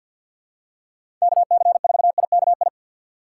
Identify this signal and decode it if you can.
Morse code, heard as a keyed tone, about 46 wpm, 690 Hz. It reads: KC4IKI
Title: USA Call Signs - 2x Character Spacing - 50wpm